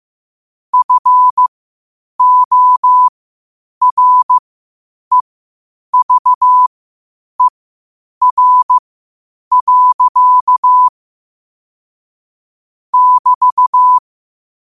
Morse code, heard as a keyed tone, about 15 wpm, spaced as for 8 wpm, 1000 Hz. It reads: FOREVER. <BT>
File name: qotdfiles / 8wpm